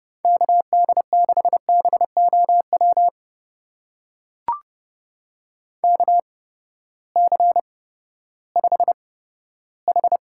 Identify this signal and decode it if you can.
Morse code, heard as a keyed tone, about 30 wpm, spaced as for 15 wpm, 700 Hz. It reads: KD6BOW KC5H
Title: USA Call Signs - 6 Characters - 8x Character Spacing - 30wpm